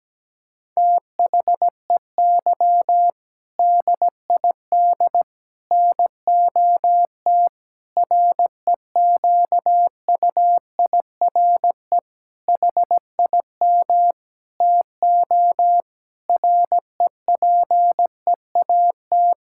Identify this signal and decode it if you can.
Morse code, heard as a keyed tone, about 17 wpm, 700 Hz. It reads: THEY DID NOT REQUIRE HIM TO REPEAT
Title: Sentences from Top 1600 Words 17wpm